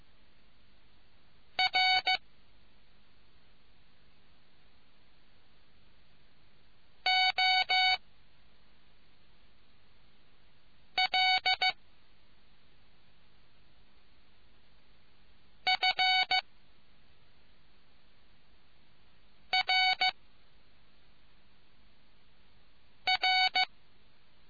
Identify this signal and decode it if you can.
Morse code, heard as a keyed tone, about 15 wpm, 750 Hz. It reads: R O L F R R